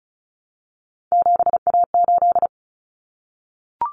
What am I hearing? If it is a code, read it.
Morse code, heard as a keyed tone, about 34 wpm, 700 Hz. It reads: 7A8
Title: Common Cut Numbers - 3-Digit 35wpm